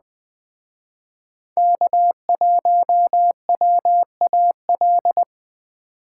Morse code, heard as a keyed tone, about 20 wpm, 700 Hz. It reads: K1WAL
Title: Speed Racing - USA Call Signs 17wpm